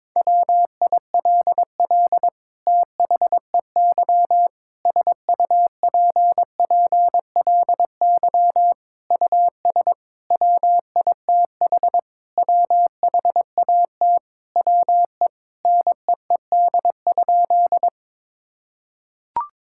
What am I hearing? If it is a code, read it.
Morse code, heard as a keyed tone, about 21 wpm, 700 Hz. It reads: WILL THEY SUPPLY US WITH WHAT WE NEED?
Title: Sentences from Top 1000 Words - Review Part 5 - 22wpm